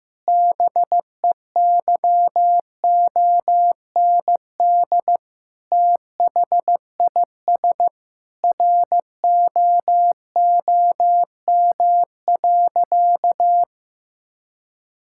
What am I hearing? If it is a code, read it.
Morse code, heard as a keyed tone, about 16 wpm, 700 Hz. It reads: BEYOND THIS ROOM.
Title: Sentences from Top 1000 Words - Review Part 6 - 15wpm